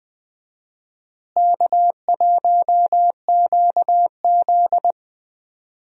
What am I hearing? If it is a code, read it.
Morse code, heard as a keyed tone, about 20 wpm, 700 Hz. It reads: K1QZ
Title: USA Call Signs - 4 Characters - Repeated 2x - 20wpm